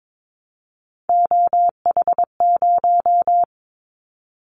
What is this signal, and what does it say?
Morse code, heard as a keyed tone, about 22 wpm, 700 Hz. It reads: OH0
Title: Callsign Prefixes for Top 100 DXCC - 22wpm